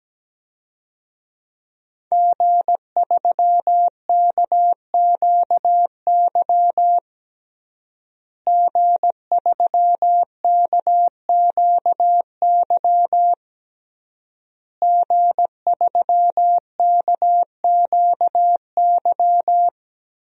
Morse code, heard as a keyed tone, about 17 wpm, 700 Hz. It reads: G3KQY G3KQY G3KQY
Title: DXCC-100 Call Signs - 5 Characters - Repeated 3x - 17wpm